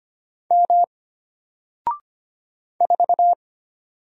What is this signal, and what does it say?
Morse code, heard as a keyed tone, about 25 wpm, 700 Hz. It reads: M 4
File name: Single-Letter-Number-25wpm